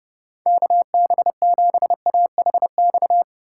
Morse code, heard as a keyed tone, about 30 wpm, 700 Hz. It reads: KB7AHX
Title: USA Call Signs - 6 Characters - 2x Character Spacing - 30wpm